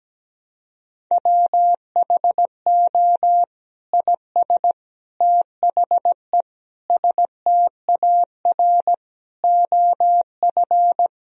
Morse code, heard as a keyed tone, about 17 wpm, 700 Hz. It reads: WHO IS THE STAR OF
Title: Sentences from Top 1000 Words - Review Part 2 - 17wpm